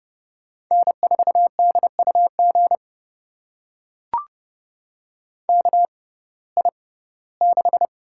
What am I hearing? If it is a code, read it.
Morse code, heard as a keyed tone, about 31 wpm, spaced as for 19 wpm, 700 Hz. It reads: N4DUG KI6
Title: USA Call Signs - 5 Characters - 6x Character Spacing - 30wpm